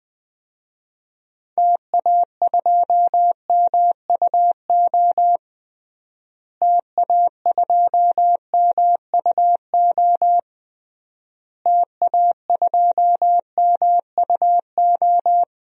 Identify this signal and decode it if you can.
Morse code, heard as a keyed tone, about 20 wpm, 700 Hz. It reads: TA2MUO TA2MUO TA2MUO
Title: DXCC-100 Call Signs - 6 Characters - Repeated 3x - 20wpm